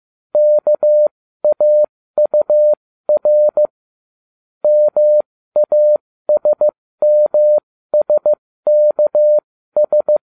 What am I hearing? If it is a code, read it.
Morse code, heard as a keyed tone, about 15 wpm, spaced as for 13 wpm, 600 Hz. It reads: KAUR MASMSKS